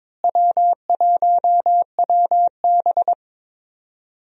Morse code, heard as a keyed tone, about 22 wpm, 700 Hz. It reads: W1WB
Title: USA Call Signs - 4 Characters - Repeated 3x - 22wpm